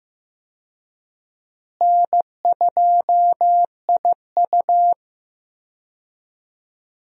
Morse code, heard as a keyed tone, about 15 wpm, 700 Hz. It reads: N2IU
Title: USA Call Signs - 4 Characters - Repeated 2x - 15wpm